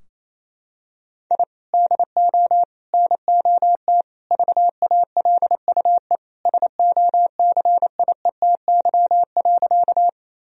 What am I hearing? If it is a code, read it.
Morse code, heard as a keyed tone, about 28 wpm, 700 Hz. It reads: I DO NOT VALUE SOCIETY.
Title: Sentences from Top 1000 Words - Review Part 4 - 28wpm